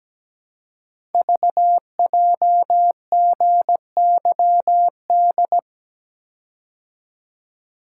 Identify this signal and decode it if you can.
Morse code, heard as a keyed tone, about 17 wpm, 700 Hz. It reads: VJGYD
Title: Code Groups - 5 Letters - 17wpm